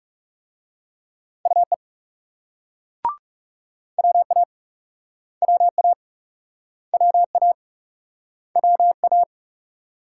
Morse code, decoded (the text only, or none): UE WA WA WA WA